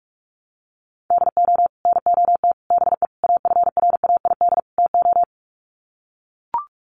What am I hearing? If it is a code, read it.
Morse code, heard as a keyed tone, about 44 wpm, 700 Hz. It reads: DO NOT BE AFRAID TO
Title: Sets of 5 Words Encore2 45wpm